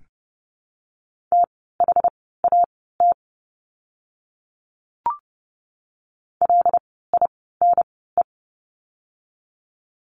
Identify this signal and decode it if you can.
Morse code, heard as a keyed tone, about 30 wpm, spaced as for 17 wpm, 700 Hz. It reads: THAT LINE
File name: course / 30:10wpm / 115 Lesson 19f - Letter w - Words 30wpm